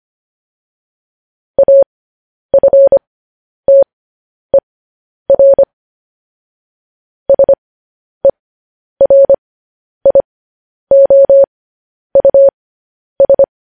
Morse code, heard as a keyed tone, about 25 wpm, spaced as for 10 wpm, 550 Hz. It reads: AFTER SERIOUS
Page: news-headlines-in-morse-code-at-10-wpm